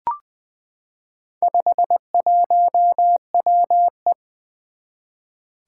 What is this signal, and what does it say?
Morse code, heard as a keyed tone, about 20 wpm, 700 Hz. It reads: 51WE
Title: Four Challenging Letter-Number Combinations 20wpm